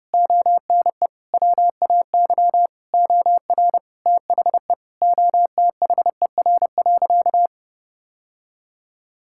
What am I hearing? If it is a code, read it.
Morse code, heard as a keyed tone, about 32 wpm, 700 Hz. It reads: ONE WAY OR THE OTHER.
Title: Sentences from Top 1000 Words - Review Part 6 - 30wpm